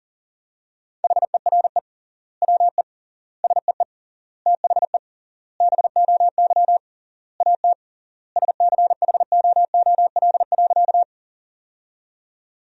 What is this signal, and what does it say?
Morse code, heard as a keyed tone, about 40 wpm, 700 Hz. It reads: HERE WE SEE THE BOY AT SCHOOL.
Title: 3x Word-Spacing - Sentences from Top 300 Words - 40wpm